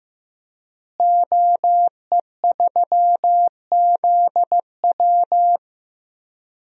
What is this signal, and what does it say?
Morse code, heard as a keyed tone, about 15 wpm, 700 Hz. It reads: OE3ZW
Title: DXCC-100 Call Signs - 5 Characters - Repeated 3x - 15wpm